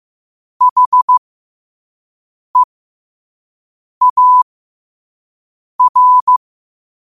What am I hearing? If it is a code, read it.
Morse code, heard as a keyed tone, about 15 wpm, spaced as for 5 wpm, 1000 Hz. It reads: HEAR
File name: qotdfiles / 5wpm